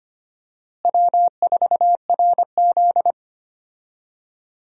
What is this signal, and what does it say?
Morse code, heard as a keyed tone, about 25 wpm, 700 Hz. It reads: W4RZ
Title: USA Call Signs - 4 Characters - Repeated 3x - 25wpm